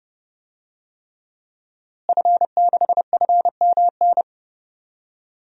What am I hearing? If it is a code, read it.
Morse code, heard as a keyed tone, about 30 wpm, 700 Hz. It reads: F6FMN
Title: DXCC-100 Call Signs - 5 Characters - Repeated 3x - 30wpm